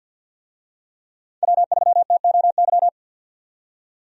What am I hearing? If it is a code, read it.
Morse code, heard as a keyed tone, about 50 wpm, 700 Hz. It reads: W3TOY
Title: USA Call Signs - 5 Characters - 50wpm